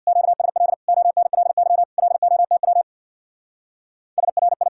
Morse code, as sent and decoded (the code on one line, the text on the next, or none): --.- ... .-.. / --- -. .-.. -.-- / .-.. --- - .-- / ... .-. ..
QSL ONLY LOTW SRI